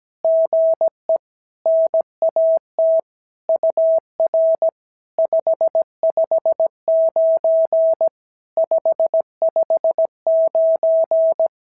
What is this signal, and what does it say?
Morse code, heard as a keyed tone, about 17 wpm, 650 Hz. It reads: GE NAT UR 559 559
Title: SOTA QSO Practice 17wpm